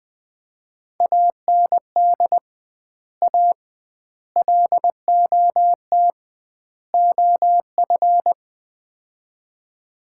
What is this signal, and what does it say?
Morse code, heard as a keyed tone, about 20 wpm, 700 Hz. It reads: AND A LOT OF
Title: 2x Word-Spacing - Sets of 4 Words - 20wpm